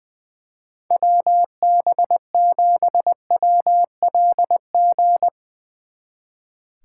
Morse code, heard as a keyed tone, about 20 wpm, 700 Hz. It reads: WB7WLG
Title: Speed Racing - USA Call Signs 20wpm